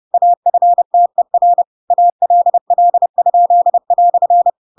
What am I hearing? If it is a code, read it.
Morse code, heard as a keyed tone, about 30 wpm, 700 Hz. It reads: AFTER ALL?"
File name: Pinocchio / Pinocchio ch8 30 30 ik1hge